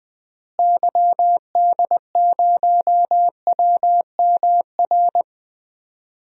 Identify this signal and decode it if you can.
Morse code, heard as a keyed tone, about 20 wpm, 700 Hz. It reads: YD0WMR
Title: DXCC-100 Call Signs - 6 Characters - Repeated 3x - 20wpm